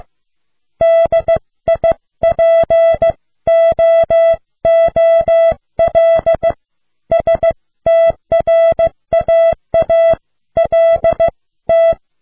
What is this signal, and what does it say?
Morse code, heard as a keyed tone, about 15 wpm, 660 Hz. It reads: DIPOOL STRAALT